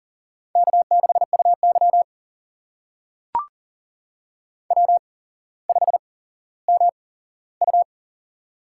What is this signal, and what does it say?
Morse code, heard as a keyed tone, about 40 wpm, spaced as for 17 wpm, 700 Hz. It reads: K6UY W5MU
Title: USA Call Signs - 4 Characters - 8x Character Spacing - 40wpm